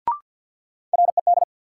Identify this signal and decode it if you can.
Morse code, heard as a keyed tone, about 54 wpm, 700 Hz. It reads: RED
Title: Sentences from Top 700 Words - New Word Drill - 50wpm